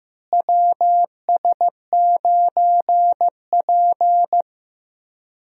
Morse code, heard as a keyed tone, about 15 wpm, 700 Hz. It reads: WS9P
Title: USA Call Signs - 4 Characters - Repeated 2x - 15wpm